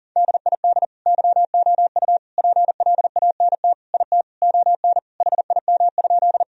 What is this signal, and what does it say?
Morse code, heard as a keyed tone, about 40 wpm, 700 Hz. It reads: DID YOU PLANT IT ON HIM?
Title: Sentences from Top 1000 Words - Review Part 1 - 40wpm